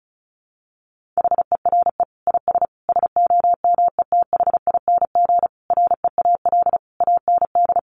Morse code, heard as a keyed tone, about 34 wpm, 700 Hz. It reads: HERE IS SOMETHING REAL AND
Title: Sentences from Top 1500 Words 35wpm